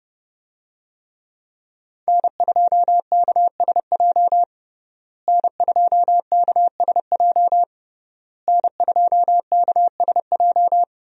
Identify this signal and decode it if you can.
Morse code, heard as a keyed tone, about 30 wpm, 700 Hz. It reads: N2KSJ N2KSJ N2KSJ
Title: USA Call Signs - 5 Characters - Repeated 3x - 30wpm